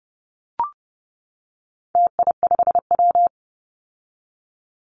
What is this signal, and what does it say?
Morse code, heard as a keyed tone, about 30 wpm, 700 Hz. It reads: TI5W
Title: Call Signs - 30wpm